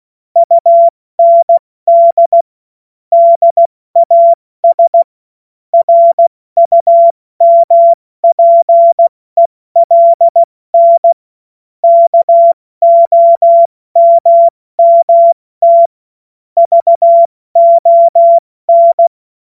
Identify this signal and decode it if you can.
Morse code, heard as a keyed tone, about 16 wpm, 680 Hz. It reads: UND DAS RUMPELN KOMMT VON
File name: Morsekurs / Witze / Witz 21. Juni 2022 70 F80